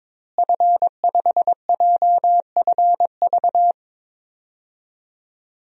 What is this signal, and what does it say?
Morse code, heard as a keyed tone, about 22 wpm, 700 Hz. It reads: F5JFV
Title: DXCC-100 Call Signs - 5 Characters - Repeated 2x - 22wpm